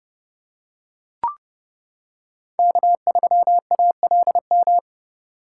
Morse code, heard as a keyed tone, about 28 wpm, 700 Hz. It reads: K3ALM